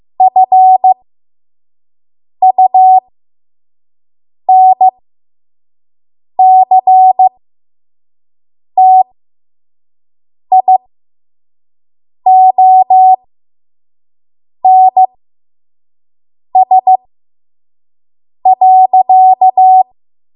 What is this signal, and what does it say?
Morse code, heard as a keyed tone, about 15 wpm, spaced as for 5 wpm, 750 Hz. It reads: FUNCTIONS.
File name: Morse / 05WPM